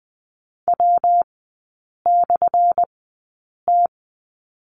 Morse code, heard as a keyed tone, about 20 wpm, spaced as for 8 wpm, 700 Hz. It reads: W/T
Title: 205 Lesson 33e - Letter j - Three Characters 20wpm